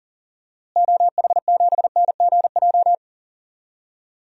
Morse code, heard as a keyed tone, about 40 wpm, 700 Hz. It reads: OH7NGJ
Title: DXCC-100 Call Signs - 6 Characters - Repeated 3x - 40wpm